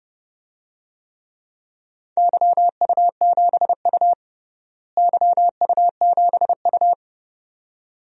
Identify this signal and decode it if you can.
Morse code, heard as a keyed tone, about 30 wpm, 700 Hz. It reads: YU7U YU7U
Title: DXCC-100 Call Signs - 4 Characters - Repeated 2x - 30wpm